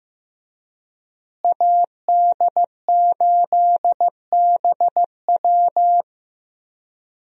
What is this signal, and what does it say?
Morse code, heard as a keyed tone, about 15 wpm, 700 Hz. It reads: AD8BW
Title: USA Call Signs - 5 Characters - Repeated 2x - 15wpm